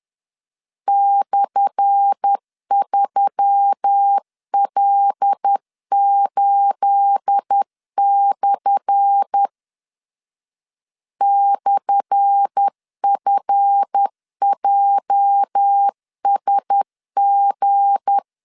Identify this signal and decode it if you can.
Morse code, heard as a keyed tone, about 10 wpm, 800 Hz. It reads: /3L8/ /FJSG
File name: LEZIONI / 10 SIGNS / 10A SLASH 10wpm 2808